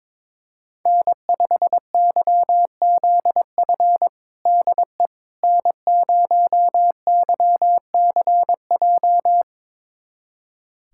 Morse code, heard as a keyed tone, about 22 wpm, 700 Hz. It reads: N5YZF DE N0YCJ
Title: USA Callsign DE Callsign - 5 Characters - 22wpm